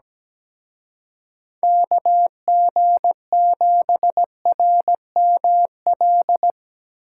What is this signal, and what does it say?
Morse code, heard as a keyed tone, about 19 wpm, 700 Hz. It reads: KG7RML
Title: Speed Racing - USA 2x3 Call Signs 17wpm